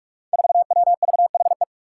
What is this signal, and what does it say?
Morse code, heard as a keyed tone, about 46 wpm, 700 Hz. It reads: 4WVHE
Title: Five Challenging Letter-Number Combinations 45wpm